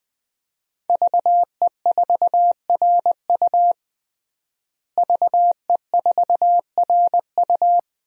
Morse code, heard as a keyed tone, about 20 wpm, 700 Hz. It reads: VE4RU VE4RU
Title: DXCC-100 Call Signs - 5 Characters - Repeated 3x - 20wpm